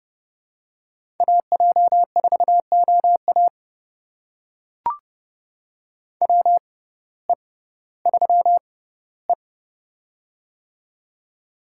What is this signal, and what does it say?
Morse code, heard as a keyed tone, about 32 wpm, spaced as for 17 wpm, 700 Hz. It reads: AJ4OA WE3E
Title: USA Call Signs - 6x Character Spacing - 30wpm